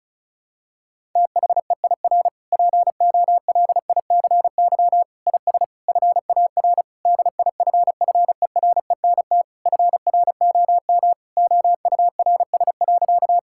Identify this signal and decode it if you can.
Morse code, heard as a keyed tone, about 34 wpm, 700 Hz. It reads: THEIR POLICY IS FAR DIFFERENT FROM OURS.
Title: Sentences from Top 1000 Words - Review Part 5 - 35wpm